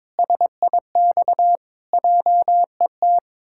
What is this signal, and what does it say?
Morse code, heard as a keyed tone, about 22 wpm, 700 Hz. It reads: SIX JET
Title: Pangram Challenge v2 - 22wpm